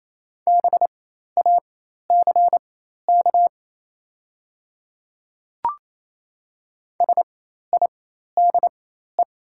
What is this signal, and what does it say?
Morse code, heard as a keyed tone, about 28 wpm, spaced as for 13 wpm, 690 Hz. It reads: BACK SIDE
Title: 4x Character Spacing - Top 200 Words - 28wpm